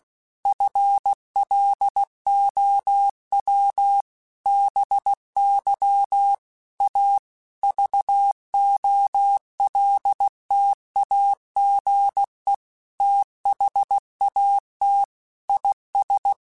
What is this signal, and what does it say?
Morse code, heard as a keyed tone, about 16 wpm, 800 Hz. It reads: FLOW BY A VOLTAGE THAT IS
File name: Technical Subjects / Technical Subjects-mp3s / XL @15